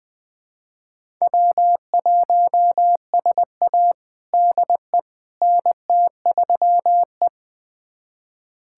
Morse code, heard as a keyed tone, about 20 wpm, 700 Hz. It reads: W1SA DE NT3E
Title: USA Callsign DE Callsign - 4 Characters - 20wpm